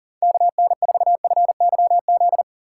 Morse code, heard as a keyed tone, about 39 wpm, 700 Hz. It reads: KN4FYZ
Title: USA Call Signs - 6 Characters - 4x Character Spacing - 40wpm